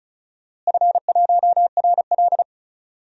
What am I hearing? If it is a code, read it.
Morse code, heard as a keyed tone, about 36 wpm, 700 Hz. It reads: F1RL
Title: DXCC-100 Call Signs - 4 Characters - Repeated 2x - 35wpm